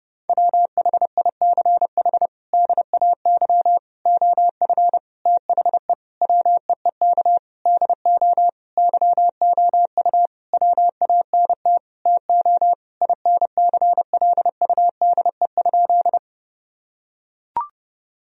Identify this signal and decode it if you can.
Morse code, heard as a keyed tone, about 32 wpm, 700 Hz. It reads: WHICH DAY OF THE WEEK DO YOU WANT TO INCLUDE?
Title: Sentences from Top 1000 Words - Review Part 3 - 30wpm